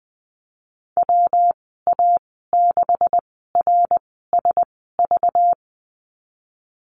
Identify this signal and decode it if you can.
Morse code, heard as a keyed tone, about 20 wpm, spaced as for 15 wpm, 700 Hz. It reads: WA6RSV
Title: 193 Lesson 31g - Letter v - Call Signs 20wpm